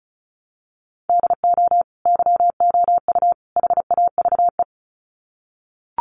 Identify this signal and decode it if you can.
Morse code, heard as a keyed tone, about 34 wpm, 700 Hz. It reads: DO YOU HAVE
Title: Sets of 3 Words 35wpm